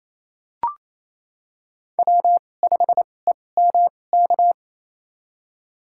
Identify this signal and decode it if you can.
Morse code, heard as a keyed tone, about 28 wpm, spaced as for 20 wpm, 700 Hz. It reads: W5EMK
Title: USA Call Signs - 5 Characters - 2x Character Spacing - 28wpm